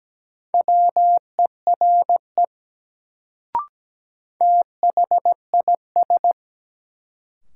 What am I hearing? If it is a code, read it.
Morse code, heard as a keyed tone, about 17 wpm, 700 Hz. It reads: WERE THIS
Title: Sentences from Top 100 Words - New Word Drill - 17wpm